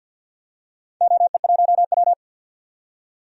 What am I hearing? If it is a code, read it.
Morse code, heard as a keyed tone, about 52 wpm, 700 Hz. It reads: OE1W
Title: DXCC-100 Call Signs - 4 Characters - 50wpm